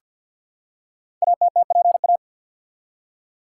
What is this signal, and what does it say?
Morse code, heard as a keyed tone, about 50 wpm, 700 Hz. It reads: ATTPA
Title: Code Groups - 5 Letters - 50wpm